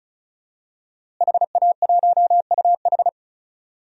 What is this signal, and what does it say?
Morse code, heard as a keyed tone, about 34 wpm, 700 Hz. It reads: HA1UH
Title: DXCC-100 Call Signs - 5 Characters - Repeated 3x - 35wpm